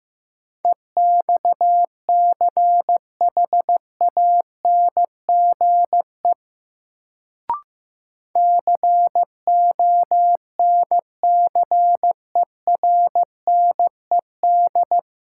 Morse code, heard as a keyed tone, about 15 wpm, 700 Hz. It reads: EXCHANGE CONCERNED
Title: Sentences from Top 1000 Words - Review Part 6 - New Word Drill - 15wpm